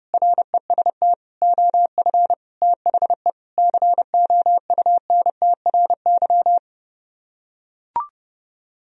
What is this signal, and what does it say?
Morse code, heard as a keyed tone, about 30 wpm, 700 Hz. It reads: REST OF THE COUNTRY